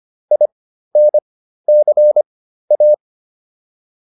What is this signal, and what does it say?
Morse code, heard as a keyed tone, about 25 wpm, spaced as for 13 wpm, 600 Hz. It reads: INCA